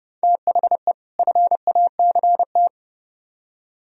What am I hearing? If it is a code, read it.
Morse code, heard as a keyed tone, about 38 wpm, 700 Hz. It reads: THE FACT